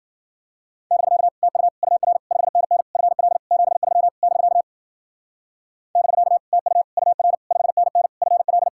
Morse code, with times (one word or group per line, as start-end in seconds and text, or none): 0.9-1.3 <BK>
1.4-1.7 TU
1.8-2.2 UR
2.3-2.8 5NN
3.0-3.4 FL
3.5-4.1 73
4.2-4.6 <BK>
6.0-6.4 <BK>
6.5-6.8 TU
7.0-7.4 UR
7.5-8.1 5NN
8.2-8.7 FL